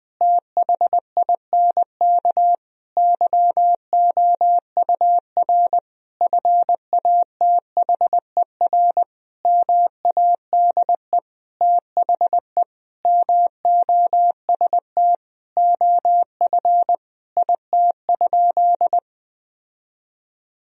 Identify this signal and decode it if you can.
Morse code, heard as a keyed tone, about 20 wpm, 700 Hz. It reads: THINK YOUR FATHER MADE THE MOST OF IT?
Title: Sentences from Top 1000 Words - Review Part 1 - 20wpm